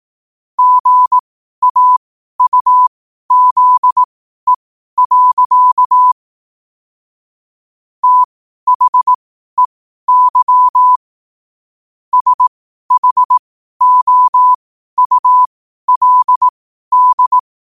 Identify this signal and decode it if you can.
Morse code, heard as a keyed tone, about 18 wpm, spaced as for 12 wpm, 1000 Hz. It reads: GAUZE. THEY SHOULD
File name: qotdfiles / 12wpm